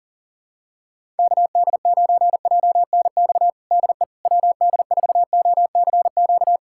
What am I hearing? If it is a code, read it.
Morse code, heard as a keyed tone, about 40 wpm, 700 Hz. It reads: KD9JNX DE WD4OCQ
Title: USA Callsign DE Callsign 40wpm